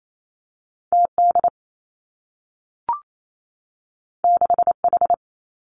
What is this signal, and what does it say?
Morse code, heard as a keyed tone, about 28 wpm, 700 Hz. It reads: T D 6 H
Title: Two Challenging Letter-Number Combinations 28wpm